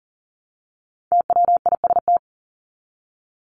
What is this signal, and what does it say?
Morse code, heard as a keyed tone, about 40 wpm, 700 Hz. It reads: TWIST
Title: Top 1751-2000 Words 40wpm